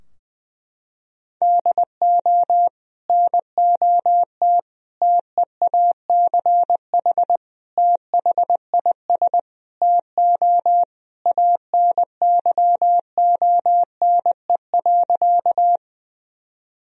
Morse code, heard as a keyed tone, about 20 wpm, 700 Hz. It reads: DO NOT TEACH THIS TO ANYONE.
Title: Sentences from Top 1000 Words - Review Part 2 - 20wpm